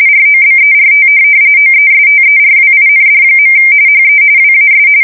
RTTY, about 45.45 baud. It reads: ROBERT OESTERWIND. HE WAS
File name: fsk170 / USS-New-Jersey